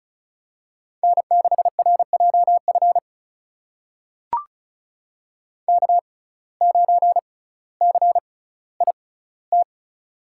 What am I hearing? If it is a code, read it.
Morse code, heard as a keyed tone, about 36 wpm, spaced as for 18 wpm, 700 Hz. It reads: N6RJF K9CIT
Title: USA Call Signs - 5 Characters - 6x Character Spacing - 35wpm